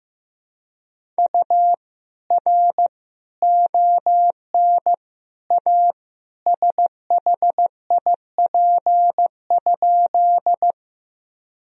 Morse code, heard as a keyed tone, about 15 wpm, 700 Hz. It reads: U R ON A SHIP?
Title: Sentences from Top 400 Words and Abbreviations 15wpm